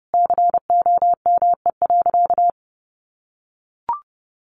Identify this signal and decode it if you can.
Morse code, heard as a keyed tone, about 30 wpm, 700 Hz. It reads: COME.
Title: Sentences from Top 1500 Words 30wpm